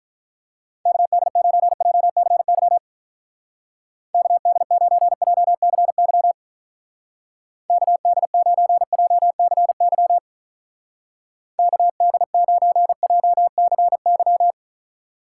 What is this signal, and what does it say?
Morse code, heard as a keyed tone, about 44 wpm, 700 Hz. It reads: KD9JCY KD9JCY KD9JCY KD9JCY
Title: Speed Racing - USA 2x3 Call Signs 35wpm